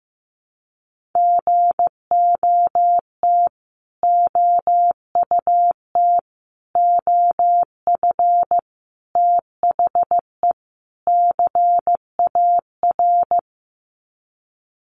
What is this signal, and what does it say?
Morse code, heard as a keyed tone, about 15 wpm, 700 Hz. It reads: GOT OUT OF THE CAR
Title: Sets of 5 Words Encore 15wpm